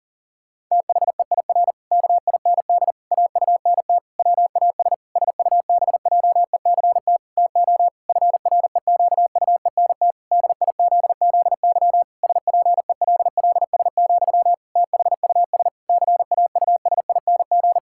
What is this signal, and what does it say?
Morse code, heard as a keyed tone, about 40 wpm, 700 Hz. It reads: THEIR KIND AUNT WAS SUBJECT TO FREQUENT DIZZY SPELLS, THUS CAUSING
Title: Pangram Challenge v2 - 40wpm